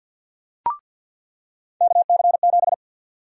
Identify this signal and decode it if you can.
Morse code, heard as a keyed tone, about 47 wpm, 700 Hz. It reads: KC7
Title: USA Call Sign Prefixes - 50wpm